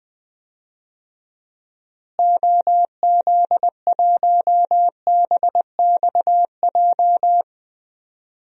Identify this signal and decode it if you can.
Morse code, heard as a keyed tone, about 20 wpm, 700 Hz. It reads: OZ1BXJ
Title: DXCC-100 Call Signs - 6 Characters - Repeated 3x - 20wpm